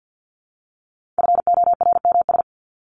Morse code, heard as a keyed tone, about 50 wpm, 700 Hz. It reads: FORMS
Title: ES:S Suffix - Top 100 Words with Base Words 50wpm